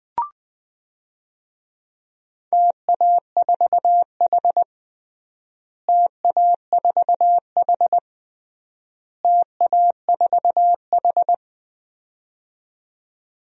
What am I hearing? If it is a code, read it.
Morse code, heard as a keyed tone, about 20 wpm, 700 Hz. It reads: TA4H TA4H TA4H
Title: DXCC-100 Call Signs - 4 Characters - Repeated 3x - 20wpm